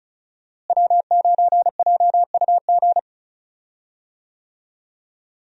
Morse code, heard as a keyed tone, about 36 wpm, 700 Hz. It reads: W9JUG
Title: USA Call Signs - 5 Characters - Repeated 3x - 35wpm